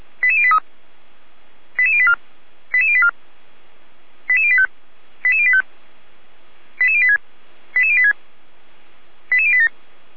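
Five-tone selective call (ZVEI), heard as quick runs of five tones, tones of about 70 ms. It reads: 88083 88084 88084 88085 88085 88086 88086 88087
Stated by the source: ZVEI five-tone call received by SDR.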